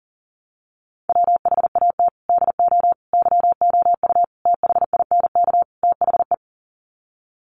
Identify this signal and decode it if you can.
Morse code, heard as a keyed tone, about 40 wpm, 700 Hz. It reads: WHAT DO YOU THINK THE
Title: Sets of 5 Words Encore 40wpm